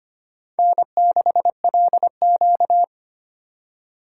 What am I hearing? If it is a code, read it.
Morse code, heard as a keyed tone, about 25 wpm, 700 Hz. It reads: N6LQ
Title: USA Call Signs - 4 Characters - 2x Character Spacing - 25wpm